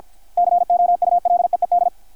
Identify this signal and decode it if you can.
Morse code, heard as a keyed tone, about 52 wpm, 660 Hz, with radio noise. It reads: COULEED
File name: archived / CRSnet